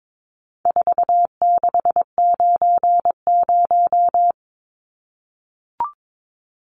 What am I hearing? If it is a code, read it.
Morse code, heard as a keyed tone, about 22 wpm, 700 Hz. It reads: 4690
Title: Numbers - 4-Digit 22wpm